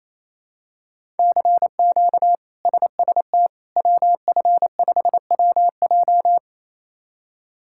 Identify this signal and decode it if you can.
Morse code, heard as a keyed tone, about 28 wpm, 700 Hz. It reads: CQ SST WF5WJ
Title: K1USN SST QSO Practice - 28wpm